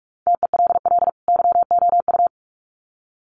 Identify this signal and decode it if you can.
Morse code, heard as a keyed tone, about 44 wpm, 700 Hz. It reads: TELL YOU
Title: Sets of 2 Words 45wpm